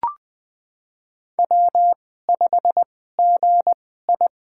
Morse code, heard as a keyed tone, about 20 wpm, spaced as for 14 wpm, 700 Hz. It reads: W5GI
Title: USA Call Signs - 4 Characters - 2x Character Spacing - 20wpm